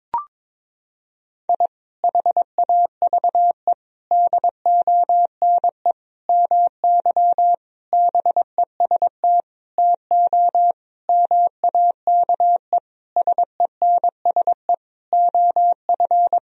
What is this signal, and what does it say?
Morse code, heard as a keyed tone, about 22 wpm, 700 Hz. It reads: I HAVE DONE MY BEST TO MAKE SENSE OF
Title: Sentences from Top 1000 Words - Review Part 4 - 22wpm